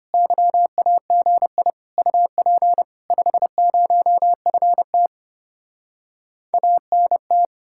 Morse code, heard as a keyed tone, about 33 wpm, 700 Hz. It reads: YAGI UP 50FT ANT